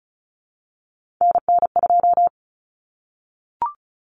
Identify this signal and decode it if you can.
Morse code, heard as a keyed tone, about 34 wpm, 700 Hz. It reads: NN2
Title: Common Cut Numbers - 3-Digit 35wpm